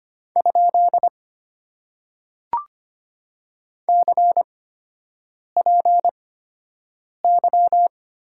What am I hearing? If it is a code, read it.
Morse code, heard as a keyed tone, about 25 wpm, spaced as for 7 wpm, 700 Hz. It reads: ? CPY
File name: Top 90 QSO Elements - 8x Character Spacing - 25wpm